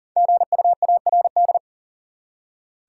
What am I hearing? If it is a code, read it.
Morse code, heard as a keyed tone, about 40 wpm, 700 Hz. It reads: GUARD
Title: Five Letter Words - 40wpm